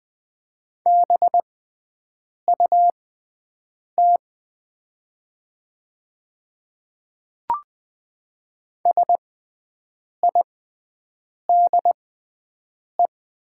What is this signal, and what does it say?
Morse code, heard as a keyed tone, about 20 wpm, spaced as for 7 wpm, 700 Hz. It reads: BUT SIDE
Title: 6x Character Spacing - Top 100 Words - 20wpm